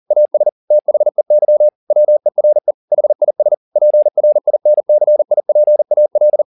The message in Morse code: AS THEY WERE HIS PRINCIPAL

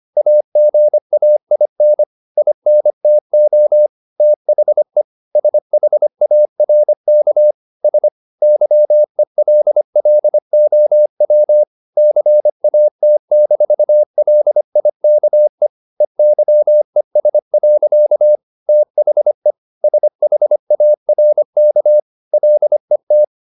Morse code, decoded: AGAIN INTO THE SHARKS YELLOW CAT-LIKE EYES. THE SHARK LET

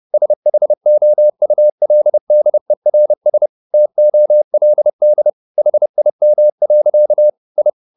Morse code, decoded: SHOULDERS TOLD HIM. I

